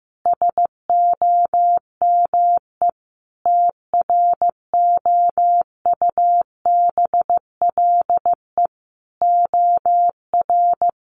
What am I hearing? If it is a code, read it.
Morse code, heard as a keyed tone, about 15 wpm, 700 Hz. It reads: SOME TROUBLE OR
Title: Sentences from Top 1500 Words 15wpm